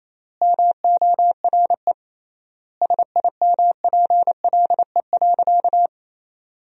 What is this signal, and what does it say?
Morse code, heard as a keyed tone, about 28 wpm, 700 Hz. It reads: MORE SIMPLE.
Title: 3x Word-Spacing - Sentences from Top 500 Words - 28wpm